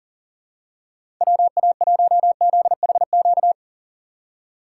Morse code, heard as a keyed tone, about 37 wpm, 700 Hz. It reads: WA1ZHQ
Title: USA Call Signs - 6 Characters - 4x Character Spacing - 40wpm